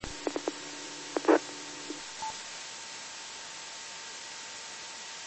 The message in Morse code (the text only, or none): none